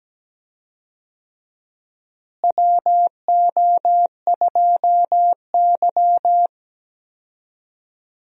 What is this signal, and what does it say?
Morse code, heard as a keyed tone, about 17 wpm, 700 Hz. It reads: WO2Y
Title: USA Call Signs - 4 Characters - Repeated 3x - 17wpm